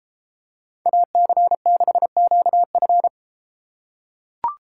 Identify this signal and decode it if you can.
Morse code, heard as a keyed tone, about 33 wpm, 700 Hz. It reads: AC6QF